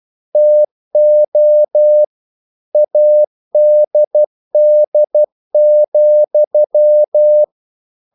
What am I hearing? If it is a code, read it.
Morse code, heard as a keyed tone, about 12 wpm, 600 Hz. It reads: TO ADD,
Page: English quotations in CW (telegraphy - morse)